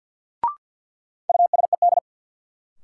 Morse code, heard as a keyed tone, about 53 wpm, 700 Hz. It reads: USED
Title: Sets of 2 Words - New Word Drill - 50wpm